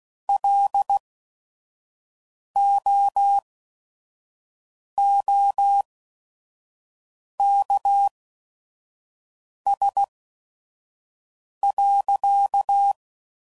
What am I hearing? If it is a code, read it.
Morse code, heard as a keyed tone, about 16 wpm, spaced as for 5 wpm, 800 Hz. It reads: LOOKS.